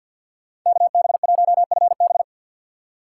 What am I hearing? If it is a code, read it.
Morse code, heard as a keyed tone, about 53 wpm, 700 Hz. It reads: KB1FB
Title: USA Call Signs - 5 Characters - 4x Character Spacing - 50wpm